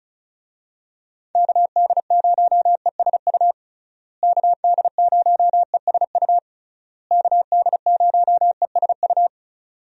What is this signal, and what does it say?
Morse code, heard as a keyed tone, about 36 wpm, 700 Hz. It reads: KD0ESU KD0ESU KD0ESU
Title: USA Call Signs - 6 Characters - Repeated 3x - 35wpm